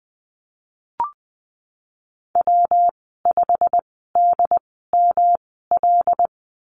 Morse code, heard as a keyed tone, about 20 wpm, spaced as for 15 wpm, 700 Hz. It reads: W5DML
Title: 179 Lesson 29g - Slash - Call Signs 20wpm